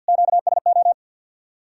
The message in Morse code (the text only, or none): QSO